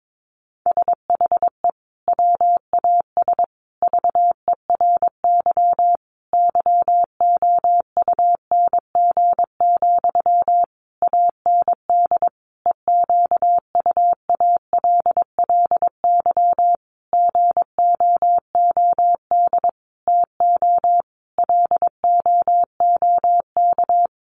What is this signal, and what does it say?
Morse code, heard as a keyed tone, about 22 wpm, 700 Hz. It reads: SHE WAS VERY YOUNG, AND EQUALLY GOOD TO LOOK
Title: Sentences from Top 1400 Words 22wpm